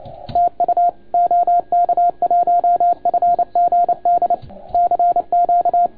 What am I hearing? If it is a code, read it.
Morse code, heard as a keyed tone, about 29 wpm, 690 Hz, with radio noise. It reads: TU OK1FGD CQ
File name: Archiv / CUC287 / OK1FGD CUC 287